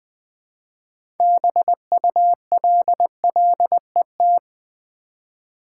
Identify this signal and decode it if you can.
Morse code, heard as a keyed tone, about 20 wpm, 700 Hz. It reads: BULLET
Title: 6 Letter Words - First 3 Letters Given - 20wpm